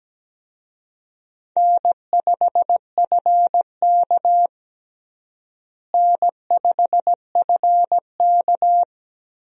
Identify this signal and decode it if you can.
Morse code, heard as a keyed tone, about 17 wpm, 700 Hz. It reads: N5FK N5FK
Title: USA Call Signs - 4 Characters - Repeated 2x - 17wpm